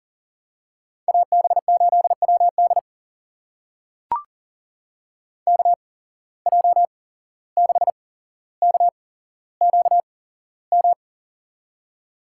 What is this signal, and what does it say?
Morse code, heard as a keyed tone, about 42 wpm, spaced as for 16 wpm, 700 Hz. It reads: AB8WD KJ6KQM
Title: USA Call Signs - 8x Character Spacing - 40wpm